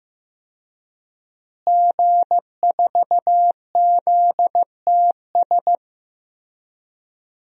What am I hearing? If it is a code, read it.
Morse code, heard as a keyed tone, about 15 wpm, 700 Hz. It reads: G4ZTS
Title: DXCC-100 Call Signs - 5 Characters - Repeated 3x - 15wpm